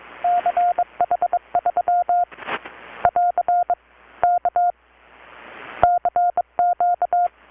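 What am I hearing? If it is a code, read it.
Morse code, heard as a keyed tone, about 23 wpm, 720 Hz. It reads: CH3 <AR> K CQ
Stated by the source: Morse from KPH's Night of Nights XX maritime radiotelegraphy event, received in Chiapas, Mexico about 3,800 km away on a Kenwood R-600 with a grounded 10 m long-wire antenna.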